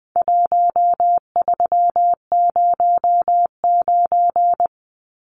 Morse code, heard as a keyed tone, about 20 wpm, 700 Hz. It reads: 1309
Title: Numbers - 4-Digit 20wpm